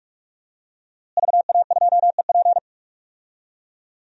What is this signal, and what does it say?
Morse code, heard as a keyed tone, about 48 wpm, 700 Hz. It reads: VA2EP
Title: DXCC-100 Call Signs - 5 Characters - Repeated 2x - 45wpm